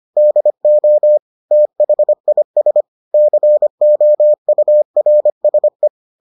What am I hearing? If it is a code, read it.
Morse code, heard as a keyed tone, about 25 wpm, 600 Hz. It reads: DO THIS COURSE